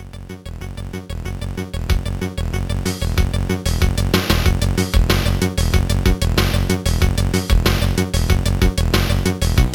There is no code.